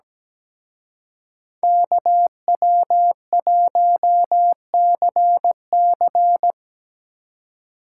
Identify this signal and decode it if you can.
Morse code, heard as a keyed tone, about 19 wpm, 700 Hz. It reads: KW1CC
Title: Speed Racing - USA 2x2 Call Signs 17wpm